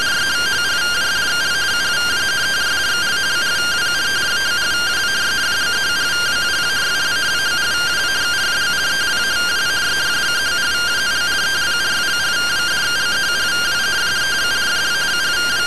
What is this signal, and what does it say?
Two-tone teleprinter signal, about 170 Hz shift.